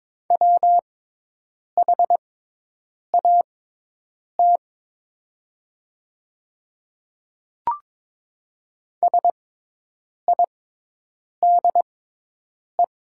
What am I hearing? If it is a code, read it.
Morse code, heard as a keyed tone, about 22 wpm, spaced as for 8 wpm, 700 Hz. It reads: WHAT SIDE
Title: 6x Character Spacing - Top 100 Words - 22wpm